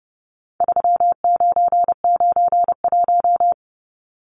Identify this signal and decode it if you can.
Morse code, heard as a keyed tone, about 30 wpm, 700 Hz. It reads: 3991
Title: Numbers - 4-Digit 30wpm